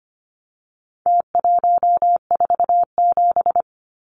Morse code, heard as a keyed tone, about 25 wpm, 700 Hz. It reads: T147
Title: Numbers - Contest Style - 4-Digit 25wpm